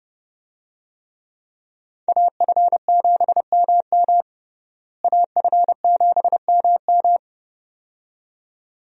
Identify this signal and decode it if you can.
Morse code, heard as a keyed tone, about 31 wpm, 700 Hz. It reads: AF7MM AF7MM
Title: USA Call Signs - 5 Characters - Repeated 2x - 30wpm